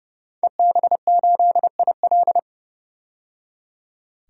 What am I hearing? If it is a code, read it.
Morse code, heard as a keyed tone, about 30 wpm, 700 Hz. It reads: EB8IL
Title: DXCC-100 Call Signs - 5 Characters - 30wpm